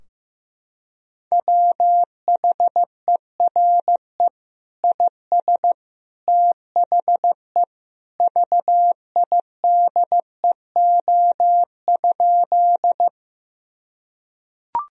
Morse code, heard as a keyed tone, about 15 wpm, 700 Hz. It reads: WHERE IS THE VIDEO?
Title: Sentences from Top 1000 Words - Review Part 4 - 15wpm